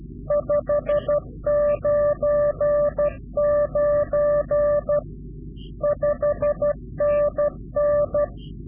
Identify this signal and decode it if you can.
Morse code, heard as a keyed tone, about 12 wpm, 590 Hz. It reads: 599 5NN
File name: CW-QSO / Directional Call / Directional QSO